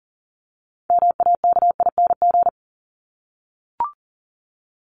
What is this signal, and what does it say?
Morse code, heard as a keyed tone, about 40 wpm, 700 Hz. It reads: MAKING
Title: ING Suffix - Top 100 Words with Base Words 40wpm